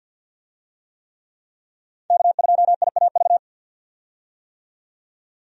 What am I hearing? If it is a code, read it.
Morse code, heard as a keyed tone, about 48 wpm, 700 Hz. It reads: K2IAV
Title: USA Call Signs - 5 Characters - 50wpm